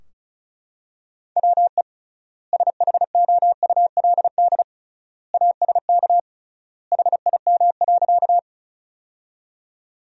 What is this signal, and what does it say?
Morse code, heard as a keyed tone, about 36 wpm, 700 Hz. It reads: WE SHOULD ASK HIM.
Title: 3x Word-Spacing - Sentences from Top 300 Words - 35wpm